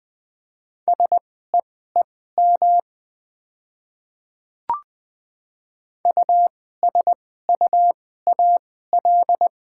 Morse code, heard as a keyed tone, about 20 wpm, spaced as for 14 wpm, 700 Hz. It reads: SEEM USUAL
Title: 2x Character Spacing - Top 500 Words - 20wpm